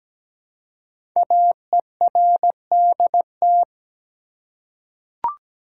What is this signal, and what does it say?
Morse code, heard as a keyed tone, about 17 wpm, 700 Hz. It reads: AERDT
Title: Code Groups - 5 Letters - 17wpm